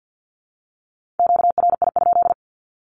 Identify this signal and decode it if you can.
Morse code, heard as a keyed tone, about 48 wpm, 700 Hz. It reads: QRI?
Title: Q-Codes Repeated 3x 50wpm